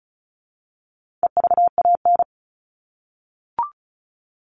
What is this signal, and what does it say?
Morse code, heard as a keyed tone, about 34 wpm, 700 Hz. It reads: EVAN
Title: All Cut Numbers - 4-Digit 35wpm